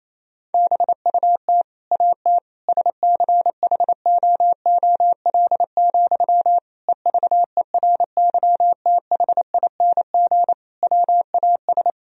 Morse code, heard as a keyed tone, about 28 wpm, 700 Hz. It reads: BUT AT SCHOOL, EVERYTHING WAS
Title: Sentences from Top 1000 Words - Review Part 3 - 28wpm